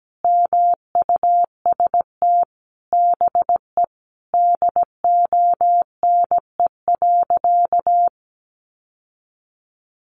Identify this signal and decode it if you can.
Morse code, heard as a keyed tone, about 17 wpm, 700 Hz. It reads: MUST BE DONE.
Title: Sentences from Top 1200 Words 17wpm